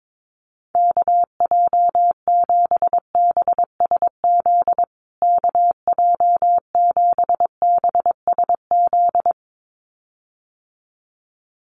Morse code, heard as a keyed tone, about 22 wpm, 700 Hz. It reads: KJ7BSZ KJ7BSZ
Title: USA Call Signs Repeated 2x 22wpm